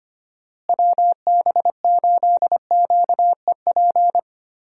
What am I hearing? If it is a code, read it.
Morse code, heard as a keyed tone, about 25 wpm, 700 Hz. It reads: WB8QEP
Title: USA Call Signs - 6 Characters - Repeated 2x - 25wpm